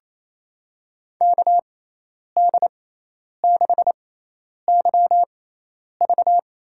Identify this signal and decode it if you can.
Morse code, heard as a keyed tone, about 28 wpm, spaced as for 10 wpm, 700 Hz. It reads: KD6YV
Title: USA Call Signs - 6 Characters - 6x Character Spacing - 28wpm